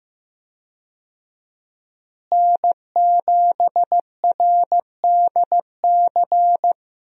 Morse code, heard as a keyed tone, about 16 wpm, 700 Hz. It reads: N7RDC
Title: USA Call Signs - 5 Characters - Repeated 2x - 15wpm